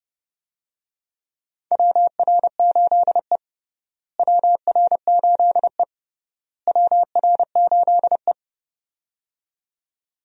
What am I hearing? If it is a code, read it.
Morse code, heard as a keyed tone, about 29 wpm, 700 Hz. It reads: WR8E WR8E WR8E
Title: USA Call Signs - 4 Characters - Repeated 3x - 30wpm